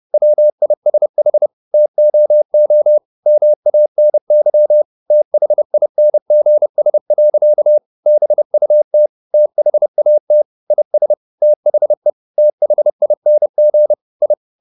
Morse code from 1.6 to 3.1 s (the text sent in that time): TOO